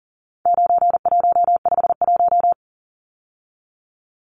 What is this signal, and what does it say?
Morse code, heard as a keyed tone, about 40 wpm, 700 Hz. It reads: 9151
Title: Numbers - 4-Digit 40wpm